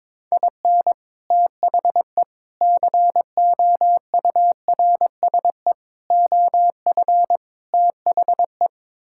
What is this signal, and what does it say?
Morse code, heard as a keyed tone, about 22 wpm, 700 Hz. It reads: IN THE COURSE OF THE
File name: practice / 22wpm / Build A Phrase - 5-Word Phrases - 22wpm